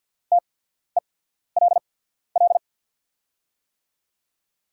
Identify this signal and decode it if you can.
Morse code, heard as a keyed tone, about 48 wpm, spaced as for 14 wpm, 700 Hz. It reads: TELL
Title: 8x Character Spacing - Top 200 Words - 50wpm